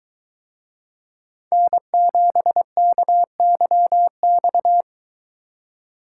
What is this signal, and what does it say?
Morse code, heard as a keyed tone, about 22 wpm, 700 Hz. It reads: N7KYX